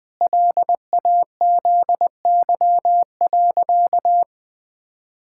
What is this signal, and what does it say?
Morse code, heard as a keyed tone, about 20 wpm, 700 Hz. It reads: LAZY.
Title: Pangram Challenge v2 - 20wpm